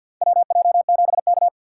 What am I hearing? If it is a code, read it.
Morse code, heard as a keyed tone, about 50 wpm, 700 Hz. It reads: WJ7K
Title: USA Call Signs - 2x Character Spacing - 50wpm